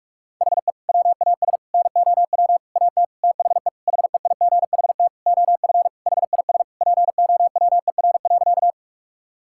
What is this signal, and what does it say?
Morse code, heard as a keyed tone, about 46 wpm, 700 Hz. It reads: HE WAS NOW AT THE HEIGHT OF HIS POWER.
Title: Sentences from Top 1600 Words 45wpm